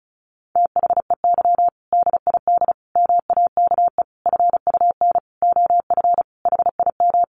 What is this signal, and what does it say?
Morse code, heard as a keyed tone, about 34 wpm, 700 Hz. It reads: THEY DID MAKE FUN OF HIM
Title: Sentences from Top 1300 Words 35wpm